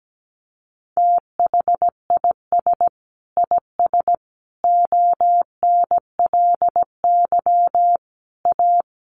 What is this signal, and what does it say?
Morse code, heard as a keyed tone, about 17 wpm, 700 Hz. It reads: THIS IS ONLY A
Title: Sentences from Top 1400 Words 17wpm